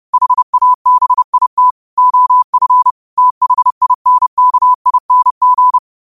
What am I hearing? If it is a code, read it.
Morse code, heard as a keyed tone, about 30 wpm, 1000 Hz. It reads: HABIT OF THINKING
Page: New quotes every day in morse code at 30 Words per minute.